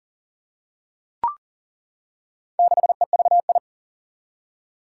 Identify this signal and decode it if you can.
Morse code, heard as a keyed tone, about 40 wpm, 700 Hz. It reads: 6EVI
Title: Four Challenging Letter-Number Combinations 40wpm